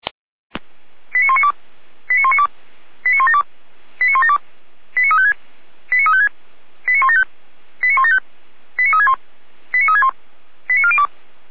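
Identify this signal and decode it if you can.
Five-tone selective call (ZVEI), heard as quick runs of five tones, tones of about 70 ms. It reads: 89192 89192 89172 89172 89356 89356 89175 89175 89361 89361 89402
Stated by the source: ZVEI five-tone call received on SDR.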